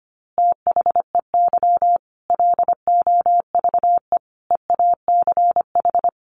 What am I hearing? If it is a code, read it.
Morse code, heard as a keyed tone, about 25 wpm, 700 Hz. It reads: THEY LOVE EACH
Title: Easy Phrases 25wpm